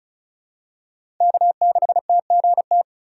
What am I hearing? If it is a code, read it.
Morse code, heard as a keyed tone, about 36 wpm, 700 Hz. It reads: K6TGT
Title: USA Call Signs - 2x Character Spacing - 35wpm